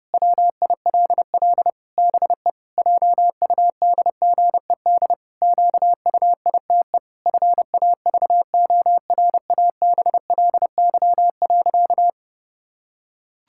Morse code, heard as a keyed tone, about 32 wpm, 700 Hz. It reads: WILL BE JUDGED QUITE FAVORABLY.
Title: Pangram Challenge v2 - 30wpm